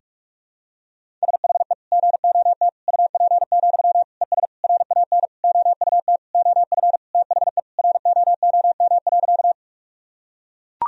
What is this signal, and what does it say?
Morse code, heard as a keyed tone, about 42 wpm, 700 Hz. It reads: SHE GOT UP, ES RAN OUT OF THE ROOM.
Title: Sentences from Top 500 Words and Abbreviations 45wpm